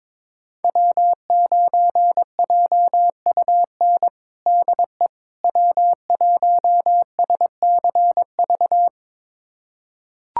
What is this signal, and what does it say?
Morse code, heard as a keyed tone, about 21 wpm, 700 Hz. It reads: W9JUN DE W1SCV
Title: USA Callsign DE Callsign - 5 Characters - 22wpm